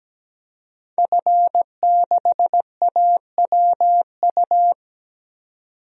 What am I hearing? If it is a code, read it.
Morse code, heard as a keyed tone, about 17 wpm, 700 Hz. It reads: F6AWU
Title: DXCC-100 Call Signs - 5 Characters - 17wpm